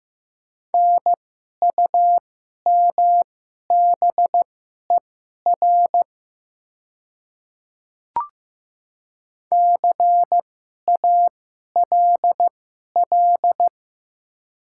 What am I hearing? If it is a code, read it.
Morse code, heard as a keyed tone, about 15 wpm, spaced as for 11 wpm, 700 Hz. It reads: NUMBER CALL
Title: 2x Character Spacing - Top 100 Words - 15wpm